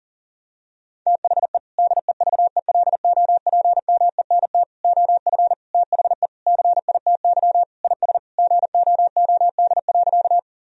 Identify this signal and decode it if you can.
Morse code, heard as a keyed tone, about 40 wpm, 700 Hz. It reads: THE DEVELOPMENT OF THE CITY IS GOOD.
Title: Sentences from Top 1000 Words - Review Part 4 - 40wpm